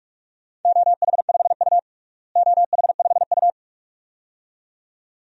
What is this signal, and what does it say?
Morse code, heard as a keyed tone, about 43 wpm, 700 Hz. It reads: OH5U OH5U
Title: DXCC-100 Call Signs - 4 Characters - Repeated 2x - 45wpm